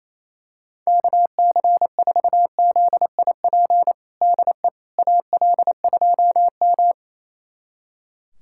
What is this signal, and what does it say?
Morse code, heard as a keyed tone, about 28 wpm, 700 Hz. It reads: KC4ZIP DE AL2M
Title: USA Callsign DE Callsign 28wpm